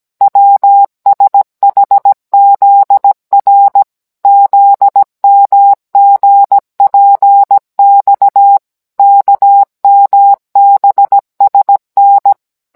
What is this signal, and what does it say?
Morse code, heard as a keyed tone, about 17 wpm, 800 Hz. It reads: WSHZR ZMGPX KMBSN